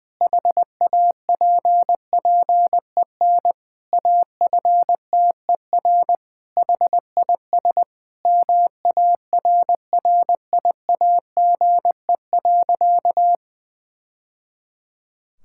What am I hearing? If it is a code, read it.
Morse code, heard as a keyed tone, about 20 wpm, 700 Hz. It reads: HAPPEN AFTER HIS MARRIAGE.
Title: Sentences from Top 1000 Words - Review Part 5 - 20wpm